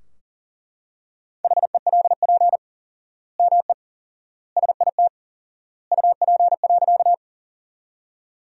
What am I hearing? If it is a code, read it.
Morse code, heard as a keyed tone, about 40 wpm, 700 Hz. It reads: HELP ME SIT UP.
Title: 4x Word-Spacing - Sentences from Top 500 Words - 40wpm